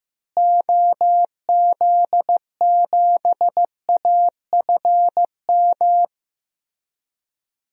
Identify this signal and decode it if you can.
Morse code, heard as a keyed tone, about 15 wpm, 700 Hz. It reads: OZ7AFM
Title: DXCC-100 Call Signs - 6 Characters - Repeated 3x - 15wpm